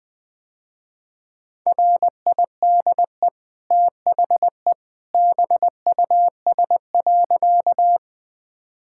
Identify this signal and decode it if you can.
Morse code, heard as a keyed tone, about 21 wpm, 700 Hz. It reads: RIDE THE BUS.
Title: Easy Sentences with Hints - 20wpm